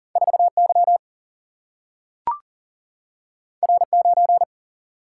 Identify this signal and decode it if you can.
Morse code, heard as a keyed tone, about 40 wpm, 700 Hz. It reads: 4 Y R 9
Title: Single-Letter Single-Number 40wpm